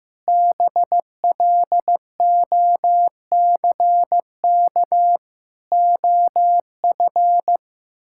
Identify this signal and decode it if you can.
Morse code, heard as a keyed tone, about 15 wpm, 700 Hz. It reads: BLOCK OF
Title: Sentences from Top 1600 Words 15wpm